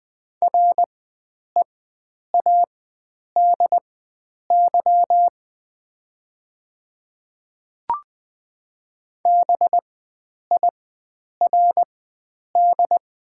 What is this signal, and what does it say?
Morse code, heard as a keyed tone, about 20 wpm, spaced as for 9 wpm, 690 Hz. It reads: READY BIRD
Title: 4x Character Spacing - Top 500 Words - 20wpm